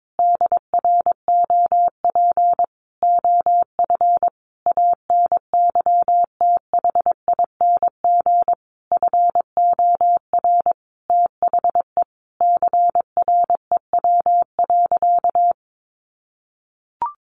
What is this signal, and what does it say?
Morse code, heard as a keyed tone, about 22 wpm, 700 Hz. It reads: DROP OF ANYTHING FOR THE CREW.
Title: Sentences from Top 1500 Words 22wpm